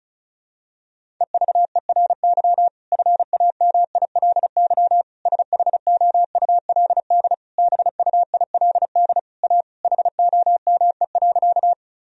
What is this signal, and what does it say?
Morse code, heard as a keyed tone, about 36 wpm, 700 Hz. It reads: EVERY FAMILY SHOULD BUILD A HOME.
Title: Sentences from Top 1000 Words - Review Part 1 - 35wpm